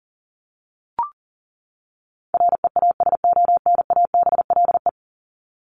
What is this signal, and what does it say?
Morse code, heard as a keyed tone, about 40 wpm, 700 Hz. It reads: REASONABLE
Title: Top 1501-1750 Words 40wpm